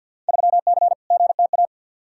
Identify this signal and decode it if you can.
Morse code, heard as a keyed tone, about 48 wpm, 700 Hz. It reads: 3C GTA
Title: Field Day Exchanges Encore - 50wpm